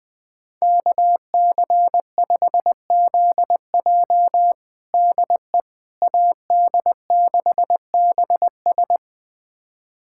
Morse code, heard as a keyed tone, about 20 wpm, 700 Hz. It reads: KC5ZJ DE AD6BS
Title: USA Callsign DE Callsign - 5 Characters - 20wpm